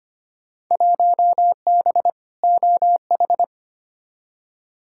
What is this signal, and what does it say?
Morse code, heard as a keyed tone, about 25 wpm, 700 Hz. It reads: 1B OH
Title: Field Day Exchanges - 25wpm